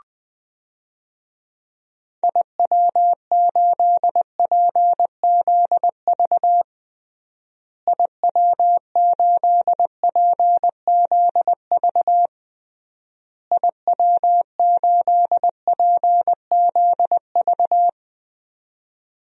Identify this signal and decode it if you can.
Morse code, heard as a keyed tone, about 20 wpm, 700 Hz. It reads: IW8PZV IW8PZV IW8PZV
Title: DXCC-100 Call Signs - 6 Characters - Repeated 3x - 20wpm